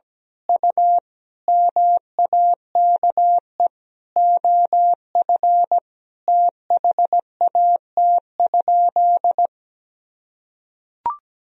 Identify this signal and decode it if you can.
Morse code, heard as a keyed tone, about 17 wpm, 700 Hz. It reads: U MAKE OF THAT?
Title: Sentences from Top 100 Words and Abbreviations 17wpm